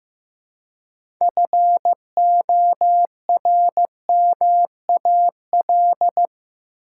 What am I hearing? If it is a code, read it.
Morse code, heard as a keyed tone, about 15 wpm, 700 Hz. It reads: FORMAL
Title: Six Letter Words - 15wpm